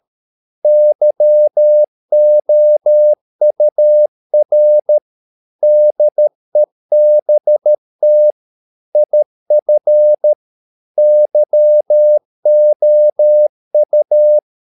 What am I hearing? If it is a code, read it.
Morse code, heard as a keyed tone, about 13 wpm, 600 Hz. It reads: YOUR DEBT IF YOU
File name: mp3 files / Lexiphiles - 13wpm